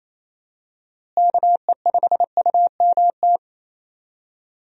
Morse code, heard as a keyed tone, about 28 wpm, 700 Hz. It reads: KE5UMT
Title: USA Call Signs - 6 Characters - Repeated 2x - 28wpm